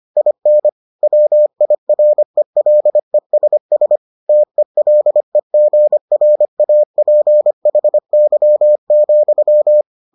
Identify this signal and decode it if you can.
Morse code, heard as a keyed tone, about 25 wpm, 600 Hz. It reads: IN WIRELESS TELEGRAPHY,